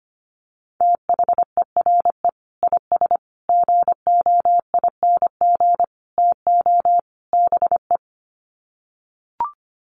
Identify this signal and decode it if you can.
Morse code, heard as a keyed tone, about 25 wpm, 700 Hz. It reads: THERE IS GOING TO BE
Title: Sets of 5 Words Encore 25wpm